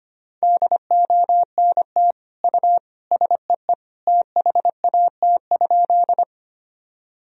Morse code, heard as a keyed tone, about 25 wpm, 700 Hz. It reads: DONT U SEE THAT?
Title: Sentences from Top 300 Words and Abbreviations 25wpm